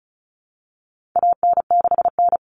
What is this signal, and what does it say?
Morse code, heard as a keyed tone, about 34 wpm, 700 Hz. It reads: AN6N
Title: Common Cut Numbers - 4-Digit 35wpm